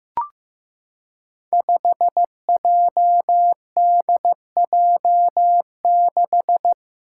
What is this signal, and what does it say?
Morse code, heard as a keyed tone, about 15 wpm, 700 Hz. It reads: 5JDJ6
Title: Five Challenging Letter-Number Combinations 15wpm